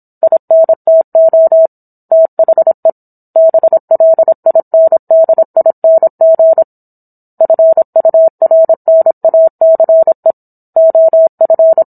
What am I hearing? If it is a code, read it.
Morse code, heard as a keyed tone, about 27 wpm, 650 Hz. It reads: INTO THE BLINDING FURNACE OF
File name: Book One / 17-Chapter 17 - 26 WPM